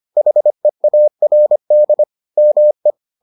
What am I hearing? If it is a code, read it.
Morse code, heard as a keyed tone, about 25 wpm, 600 Hz. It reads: HEARD ME